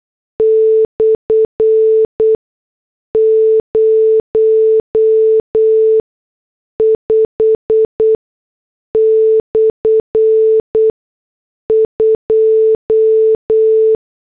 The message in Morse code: /05/2